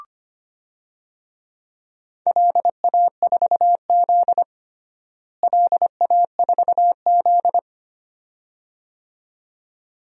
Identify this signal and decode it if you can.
Morse code, heard as a keyed tone, about 25 wpm, 700 Hz. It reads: LA4Z LA4Z
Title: DXCC-100 Call Signs - 4 Characters - Repeated 2x - 25wpm